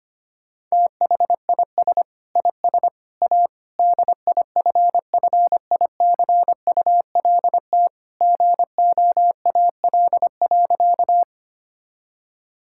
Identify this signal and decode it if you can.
Morse code, heard as a keyed tone, about 25 wpm, 700 Hz. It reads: THIS IS A DIFFICULT GOAL.
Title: Sentences from Top 1000 Words - Review Part 5 - 25wpm